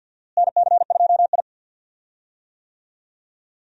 Morse code, heard as a keyed tone, about 51 wpm, 700 Hz. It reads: NC2I
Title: USA Call Signs - 4 Characters - Repeated 2x - 50wpm